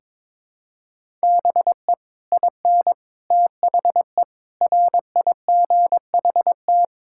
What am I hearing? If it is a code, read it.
Morse code, heard as a keyed tone, about 22 wpm, 700 Hz. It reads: BE IN THE RIGHT
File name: Sets of 9 Words - 22wpm